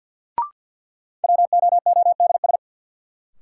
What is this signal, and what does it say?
Morse code, heard as a keyed tone, about 53 wpm, 700 Hz. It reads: WOODS
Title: Sentences from Top 1000 Words - Review Part 2 - New Word Drill - 50wpm